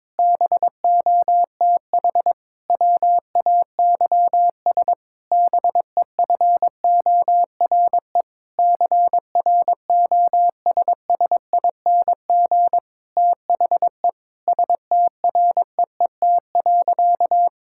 Morse code, heard as a keyed tone, about 22 wpm, 700 Hz. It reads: BOTH WAYS BEFORE CROSSING THE STREET.
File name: Difficult Sentences with Hints - 22wpm